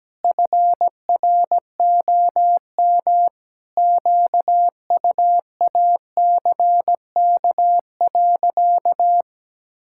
Morse code, heard as a keyed tone, about 17 wpm, 700 Hz. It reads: FROM QUACK.
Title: Pangram Challenge v2 - 17wpm